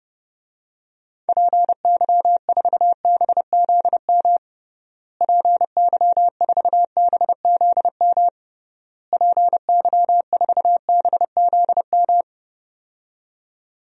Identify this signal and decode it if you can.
Morse code, heard as a keyed tone, about 32 wpm, 700 Hz. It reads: PY4BZM PY4BZM PY4BZM
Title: DXCC-100 Call Signs - 6 Characters - Repeated 3x - 30wpm